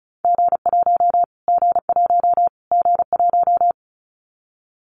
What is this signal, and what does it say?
Morse code, heard as a keyed tone, about 34 wpm, 700 Hz. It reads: G1 G1 G1
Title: Callsign Prefixes for Top 25 DXCC - Repeated 3x - 35wpm